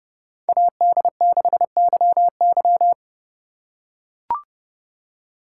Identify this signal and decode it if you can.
Morse code, heard as a keyed tone, about 28 wpm, 700 Hz. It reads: AD6YY